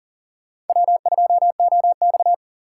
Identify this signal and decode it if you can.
Morse code, heard as a keyed tone, about 40 wpm, 700 Hz. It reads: W2OX
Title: USA Call Signs - 3x Character Spacing - 40wpm